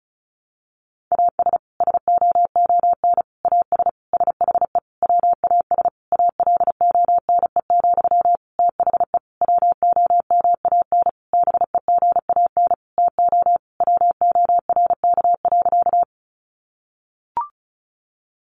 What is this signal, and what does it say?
Morse code, heard as a keyed tone, about 34 wpm, 700 Hz. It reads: AS SOON AS SHE WAS ALONE, THE WOMAN BEGAN TO WORK.
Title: Sentences from Top 1100 Words 35wpm